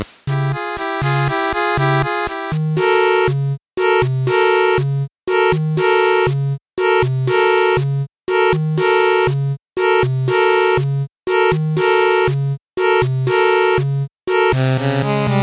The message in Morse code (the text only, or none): none